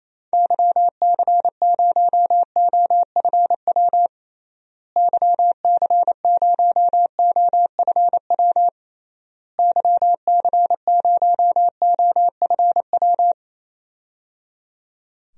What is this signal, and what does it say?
Morse code, heard as a keyed tone, about 28 wpm, 700 Hz. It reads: YC0OFW YC0OFW YC0OFW
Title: DXCC-100 Call Signs - 6 Characters - Repeated 3x - 28wpm